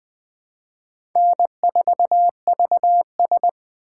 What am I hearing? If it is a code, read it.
Morse code, heard as a keyed tone, about 20 wpm, 700 Hz. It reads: N4VS
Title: USA Call Signs - 4 Characters - 6x Character Spacing - 20wpm